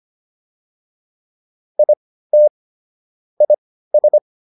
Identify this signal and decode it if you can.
Morse code, heard as a keyed tone, about 25 wpm, spaced as for 15 wpm, 600 Hz. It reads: IT IS